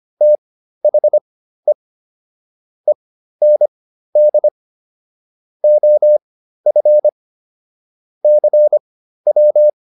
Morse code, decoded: THE END OF CW

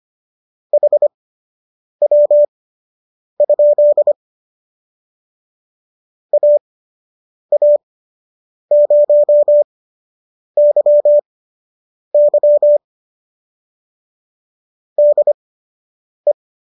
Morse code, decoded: HW? AA0YY DE

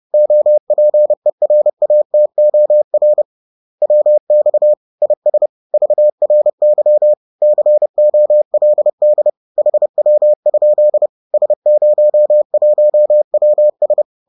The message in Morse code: OPERATOR WX IS VRY COLD HW? S01WS

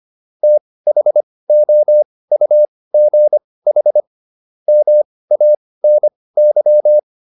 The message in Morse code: THOUGH MANY

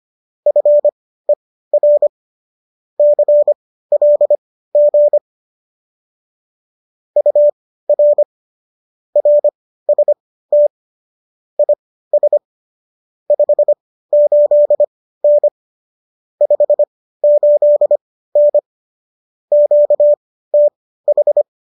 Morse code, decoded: FER CLG UR RST IS 58N 58N QTH

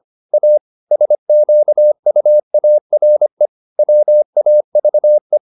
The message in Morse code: A SQUARE WAVE

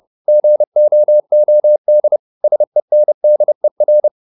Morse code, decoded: GOOD SENDER

